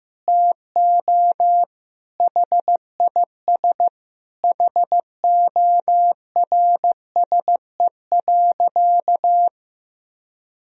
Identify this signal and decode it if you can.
Morse code, heard as a keyed tone, about 15 wpm, 700 Hz. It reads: TO HIS HORSE.
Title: Sentences from Top 1000 Words - Review Part 1 - 15wpm